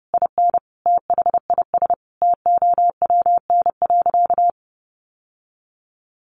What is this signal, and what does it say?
Morse code, heard as a keyed tone, about 30 wpm, 700 Hz. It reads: IN THIS TOWN.
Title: Sentences from Top 1000 Words 30wpm